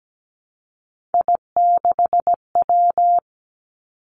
Morse code, heard as a keyed tone, about 17 wpm, 700 Hz. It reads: I6W
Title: Three Challenging Letter-Number Combinations 17wpm